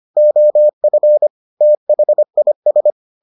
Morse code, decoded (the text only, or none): OF THIS